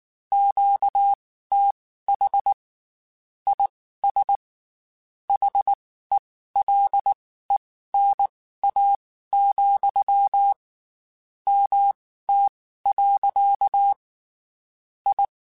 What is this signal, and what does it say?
Morse code, heard as a keyed tone, about 19 wpm, spaced as for 14 wpm, 800 Hz. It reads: QTH IS HELENA, MT. I
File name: QSOs / QSOs-mp3s / WD0BGP 1@18